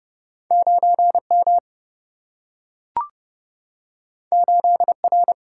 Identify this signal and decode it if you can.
Morse code, heard as a keyed tone, about 30 wpm, 700 Hz. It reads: 9 M 8 R
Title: Single-Letter Single-Number 30wpm